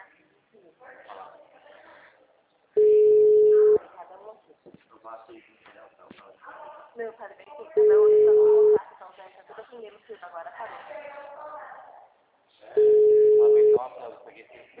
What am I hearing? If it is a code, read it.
Ringback tone (Europe, 425 Hz), with speech.
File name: out- -2010- - - .19972